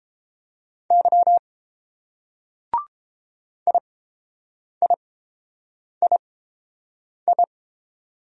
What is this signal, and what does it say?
Morse code, heard as a keyed tone, about 30 wpm, spaced as for 7 wpm, 700 Hz. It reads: Y IIII